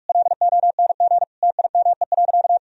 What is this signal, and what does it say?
Morse code, heard as a keyed tone, about 46 wpm, 700 Hz. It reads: LONG TIME.
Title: Sentences from Top 1000 Words - Review Part 4 - 45wpm